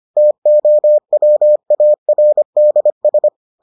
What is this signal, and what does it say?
Morse code, heard as a keyed tone, about 25 wpm, 600 Hz. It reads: TOWARDS